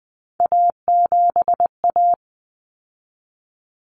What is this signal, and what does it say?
Morse code, heard as a keyed tone, about 20 wpm, 700 Hz. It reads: A7A
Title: Common Cut Numbers - 3-Digit 20wpm